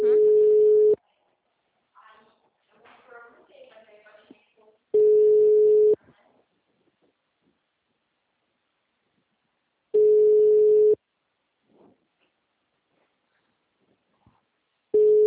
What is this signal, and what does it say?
Ringback tone (Europe, 425 Hz), with speech.